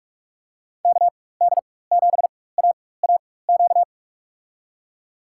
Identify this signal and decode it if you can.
Morse code, heard as a keyed tone, about 46 wpm, spaced as for 21 wpm, 700 Hz. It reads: KD7AAQ
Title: USA Call Signs - 4x Character Spacing - 45wpm